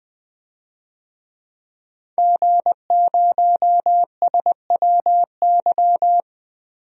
Morse code, heard as a keyed tone, about 20 wpm, 700 Hz. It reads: G0SWY
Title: DXCC-100 Call Signs - 5 Characters - Repeated 2x - 20wpm